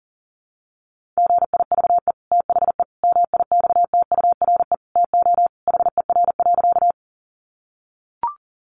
Morse code, heard as a keyed tone, about 40 wpm, 700 Hz. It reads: GIVE THE MIXTURE TO HER.
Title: Sentences from Top 1500 Words 40wpm